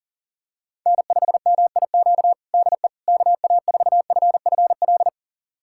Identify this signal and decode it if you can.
Morse code, heard as a keyed tone, about 40 wpm, 700 Hz. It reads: N5MIQ DE KA4FFL
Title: USA Callsign DE Callsign 40wpm